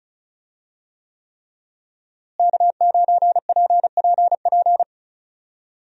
Morse code, heard as a keyed tone, about 34 wpm, 700 Hz. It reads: K9PPP
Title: USA Call Signs - 5 Characters - Repeated 3x - 35wpm